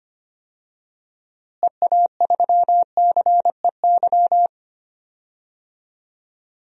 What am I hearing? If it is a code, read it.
Morse code, heard as a keyed tone, about 25 wpm, 700 Hz. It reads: EA3CEY
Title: DXCC-100 Call Signs - 6 Characters - Repeated 2x - 25wpm